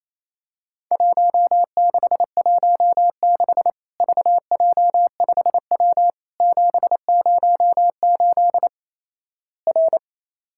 Morse code, heard as a keyed tone, about 28 wpm, 700 Hz. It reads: 1616 VJ5W 708 R
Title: Worked All Europe DX Contest - QTC Exchange - 28wpm